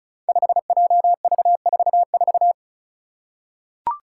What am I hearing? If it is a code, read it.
Morse code, heard as a keyed tone, about 36 wpm, 700 Hz. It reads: 5JV44
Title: Five Challenging Letter-Number Combinations 35wpm